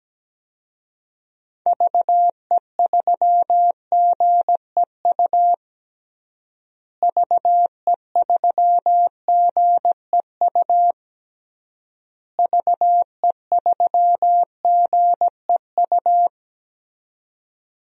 Morse code, heard as a keyed tone, about 17 wpm, 700 Hz. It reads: VE3GEU VE3GEU VE3GEU
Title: DXCC-100 Call Signs - 6 Characters - Repeated 3x - 17wpm